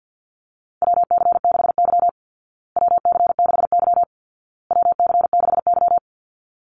Morse code, heard as a keyed tone, about 50 wpm, 700 Hz. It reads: WC6Y WC6Y WC6Y
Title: Call Signs Repeated 2x - 50wpm